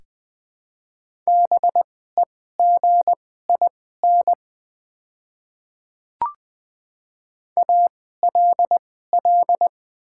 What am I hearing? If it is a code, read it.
Morse code, heard as a keyed tone, about 21 wpm, spaced as for 15 wpm, 700 Hz. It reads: BEGIN ALL
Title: 2x Character Spacing - Top 500 Words - 20wpm